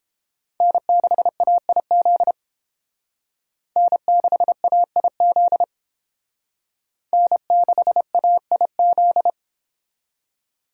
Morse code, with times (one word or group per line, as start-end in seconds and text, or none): 0.6-2.3 N6AIZ
3.8-5.6 N6AIZ
7.1-9.3 N6AIZ